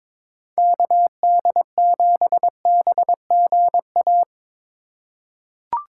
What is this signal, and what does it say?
Morse code, heard as a keyed tone, about 21 wpm, 700 Hz. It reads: KD7BGA
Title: USA Call Signs - 2x Character Spacing - 22wpm